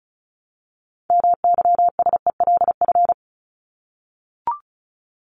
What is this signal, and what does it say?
Morse code, heard as a keyed tone, about 34 wpm, 700 Hz. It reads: MYSELF
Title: Top 1001-1250 Words 35wpm